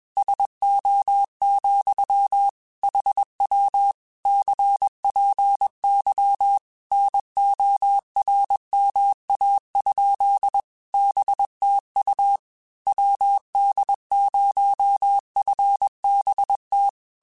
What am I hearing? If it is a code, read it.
Morse code, heard as a keyed tone, about 21 wpm, 800 Hz. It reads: SO, HW CPY NORMA? BTU WD0FBT
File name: QSOs / QSOs-mp3s / WD0FBT 1@20